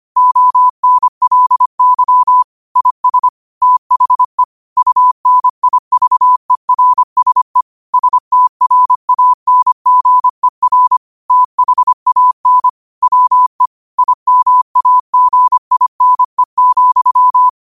Morse code, heard as a keyed tone, about 25 wpm, 1000 Hz. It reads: ONLY IS THE UNIVERSE STRANGER THAN WE IMAGINE,
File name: qotdfiles / 25wpm